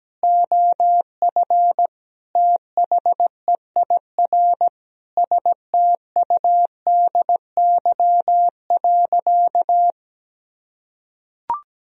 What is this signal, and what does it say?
Morse code, heard as a keyed tone, about 17 wpm, 700 Hz. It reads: OF THEIR STUDY.
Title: Sentences from Top 1000 Words - Review Part 3 - 17wpm